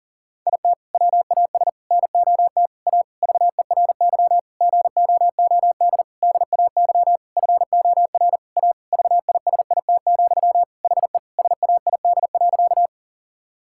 Morse code, heard as a keyed tone, about 40 wpm, 700 Hz. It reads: IT WAS NOT A VERY GOOD DAY FOR A VISIT, HE SAID.
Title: Sentences from Top 1000 Words - Review Part 6 - 40wpm